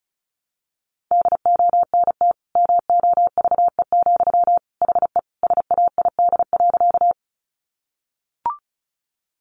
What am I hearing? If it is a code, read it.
Morse code, heard as a keyed tone, about 34 wpm, 700 Hz. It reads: DONT MOVE, HE SAID.
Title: Sentences from Top 300 Words 35wpm